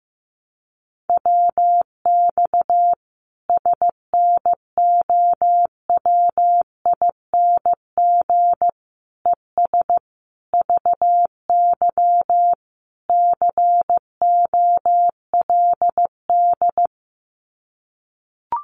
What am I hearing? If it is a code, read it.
Morse code, heard as a keyed tone, about 15 wpm, 700 Hz. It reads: WX SNOWING ES VY COLD
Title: Common QSO Phrases 15wpm